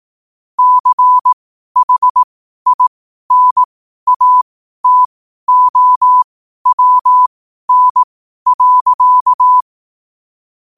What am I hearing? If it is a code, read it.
Morse code, heard as a keyed tone, about 18 wpm, spaced as for 12 wpm, 1000 Hz. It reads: CHINATOWN.
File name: qotdfiles / 12wpm